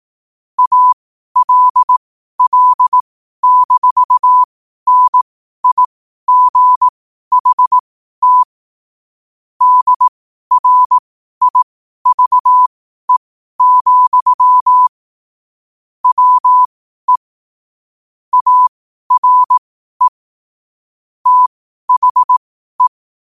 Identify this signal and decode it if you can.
Morse code, heard as a keyed tone, about 18 wpm, spaced as for 12 wpm, 1000 Hz. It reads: ALL-NIGHT DRIVE, WE ARE THE